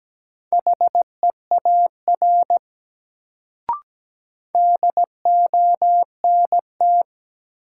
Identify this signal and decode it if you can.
Morse code, heard as a keyed tone, about 17 wpm, 700 Hz. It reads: HEAR DONT
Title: Sentences from Top 300 Words - New Word Drill - 17wpm